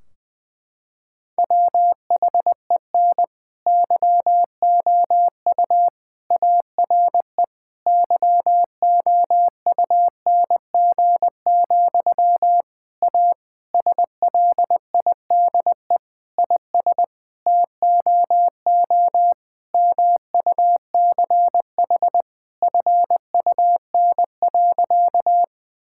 Morse code, heard as a keyed tone, about 20 wpm, 700 Hz. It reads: WHEN YOU ARE YOUNG, A SLIDE IS TOO MUCH FUN.
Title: Sentences from Top 1600 Words 20wpm